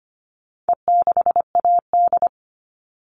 Morse code, heard as a keyed tone, about 25 wpm, 700 Hz. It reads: E6AD
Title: All Cut Numbers - 4-Digit 25wpm